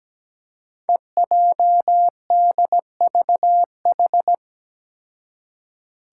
Morse code, heard as a keyed tone, about 17 wpm, 700 Hz. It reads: EJDVH
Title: Five Challenging Letter-Number Combinations 17wpm